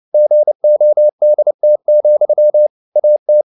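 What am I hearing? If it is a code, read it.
Morse code, heard as a keyed tone, about 29 wpm, 600 Hz. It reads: GODT, AT
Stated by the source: Fast (QRQ) Morse at 29 wpm, from CW med Gnister.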